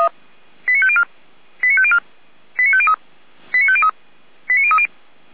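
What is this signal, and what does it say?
Five-tone selective call (ZVEI), heard as quick runs of five tones, tones of about 80 ms. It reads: 89504 89504 89503 89503 89030
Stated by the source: ZVEI five-tone call received by SDR, archived by Feuerwehr Orlamünde.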